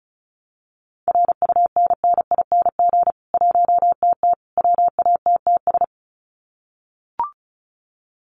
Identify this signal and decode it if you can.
Morse code, heard as a keyed tone, about 34 wpm, 700 Hz. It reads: RUNNING 1TT WATTS
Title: Common QSO Phrases 35wpm